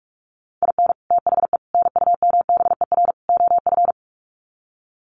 Morse code, heard as a keyed tone, about 44 wpm, 700 Hz. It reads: IN THE NUMBER OF
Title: Sets of 4 Words 45wpm